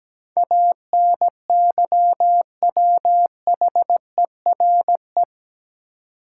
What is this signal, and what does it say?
Morse code, heard as a keyed tone, about 17 wpm, 700 Hz. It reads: ANYWHERE
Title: Sentences from Top 1400 Words - New Word Drill - 17wpm